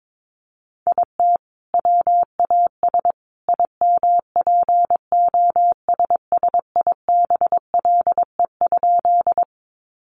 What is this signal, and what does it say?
Morse code, heard as a keyed tone, about 22 wpm, 700 Hz. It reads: IT WAS IMPOSSIBLE?
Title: Sentences from Top 1100 Words 22wpm